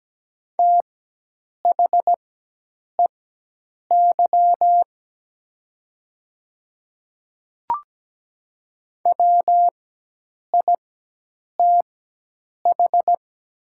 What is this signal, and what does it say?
Morse code, heard as a keyed tone, about 17 wpm, spaced as for 8 wpm, 700 Hz. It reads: THEY WITH
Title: 4x Character Spacing - Top 100 Words - 17wpm